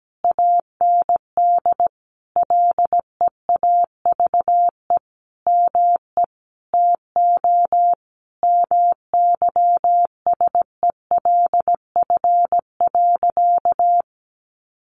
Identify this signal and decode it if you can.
Morse code, heard as a keyed tone, about 17 wpm, 700 Hz. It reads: AND LEAVE ME TO MYSELF.
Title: Sentences from Top 1300 Words 17wpm